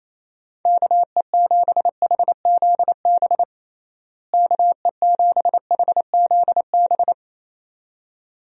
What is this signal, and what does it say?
Morse code, heard as a keyed tone, about 28 wpm, 700 Hz. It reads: KE7HZB KE7HZB
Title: USA Call Signs - 6 Characters - Repeated 2x - 28wpm